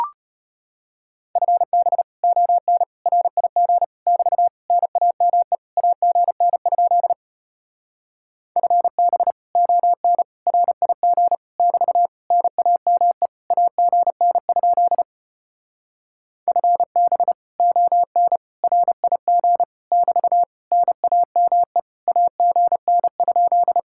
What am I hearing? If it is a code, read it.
Morse code, heard as a keyed tone, about 33 wpm, 700 Hz. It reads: FB ON RIG <BT> NAME AGN? FB ON RIG <BT> NAME AGN? FB ON RIG <BT> NAME AGN?